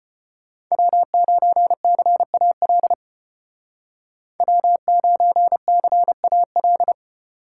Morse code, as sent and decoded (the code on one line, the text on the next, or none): .-- ----. -.-. .- .-.. / .-- ----. -.-. .- .-..
W9CAL W9CAL